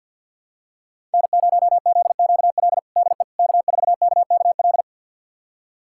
Morse code, heard as a keyed tone, about 50 wpm, 700 Hz. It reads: N0ZQL DE K4KKL
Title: USA Callsign DE Callsign - 5 Characters - 50wpm